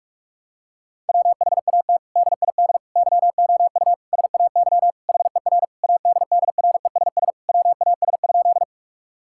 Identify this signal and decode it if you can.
Morse code, heard as a keyed tone, about 46 wpm, 700 Hz. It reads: WHAT DID YOU SAY HER ADDRESS WAS?
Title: Sentences from Top 1000 Words - Review Part 6 - 45wpm